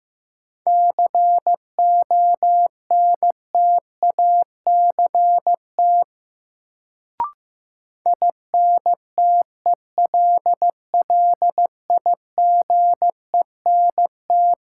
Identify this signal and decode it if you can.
Morse code, heard as a keyed tone, about 15 wpm, 700 Hz. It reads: CONTACT INTELLIGENT
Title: Sentences from Top 1000 Words - Review Part 7 - New Word Drill - 15wpm